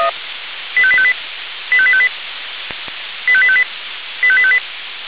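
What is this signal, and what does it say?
Five-tone selective call (ZVEI), heard as quick runs of five tones, tones of about 70 ms. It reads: 85858 85858 85858 85858
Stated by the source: ZVEI five-tone selective call received with an SDR.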